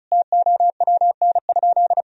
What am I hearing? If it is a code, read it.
Morse code, heard as a keyed tone, about 36 wpm, 700 Hz. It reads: TOWN?
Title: Sentences from Top 1000 Words - Review Part 2 - 35wpm